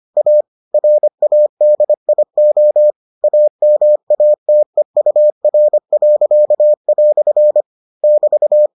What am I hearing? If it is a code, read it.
Morse code, heard as a keyed tone, about 25 wpm, 600 Hz. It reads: A RADIO AMATEUR." <BT>